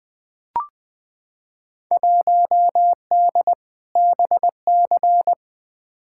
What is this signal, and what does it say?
Morse code, heard as a keyed tone, about 20 wpm, 700 Hz. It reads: 1D BC
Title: Field Day Exchanges - 20wpm